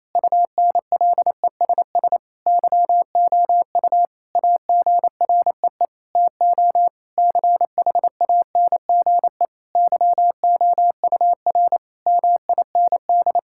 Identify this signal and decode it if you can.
Morse code, heard as a keyed tone, about 28 wpm, 700 Hz. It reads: UNLESS YOU AGREE TO CHANGE YOUR MIND
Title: Sentences from Top 1000 Words - Review Part 6 - 28wpm